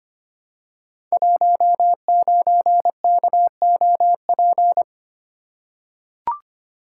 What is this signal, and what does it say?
Morse code, heard as a keyed tone, about 25 wpm, 700 Hz. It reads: 19KOP
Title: Code Groups - 5 Letter:Numbers - 25wpm